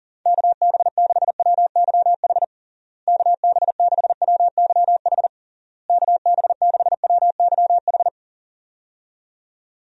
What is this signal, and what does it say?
Morse code, heard as a keyed tone, about 37 wpm, 700 Hz. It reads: KB6WYH KB6WYH KB6WYH
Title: USA Call Signs - 6 Characters - Repeated 3x - 40wpm